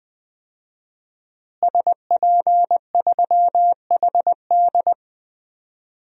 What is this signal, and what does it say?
Morse code, heard as a keyed tone, about 20 wpm, 700 Hz. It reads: SP3HD
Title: DXCC-100 Call Signs - 5 Characters - Repeated 3x - 20wpm